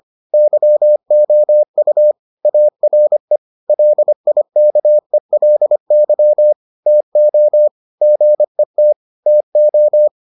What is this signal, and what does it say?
Morse code, heard as a keyed tone, about 25 wpm, 600 Hz. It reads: YOU ARE LIKELY TO GET TO